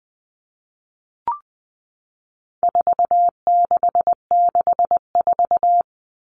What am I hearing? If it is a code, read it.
Morse code, heard as a keyed tone, about 20 wpm, 700 Hz. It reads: 4664
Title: All Cut Numbers - 4-Digit 20wpm